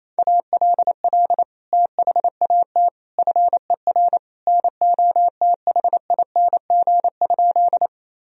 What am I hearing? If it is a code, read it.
Morse code, heard as a keyed tone, about 28 wpm, 700 Hz. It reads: ALL THAT FER NOTHING?
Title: Sentences from Top 500 Words and Abbreviations 28wpm